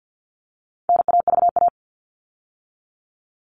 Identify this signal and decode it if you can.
Morse code, heard as a keyed tone, about 48 wpm, 700 Hz. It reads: NAVA
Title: All Cut Numbers - 4-Digit 50wpm